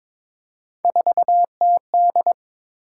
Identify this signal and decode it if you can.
Morse code, heard as a keyed tone, about 22 wpm, 700 Hz. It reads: 4TD
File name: practice / 22wpm / Three Letter-Number Combinations - 22wpm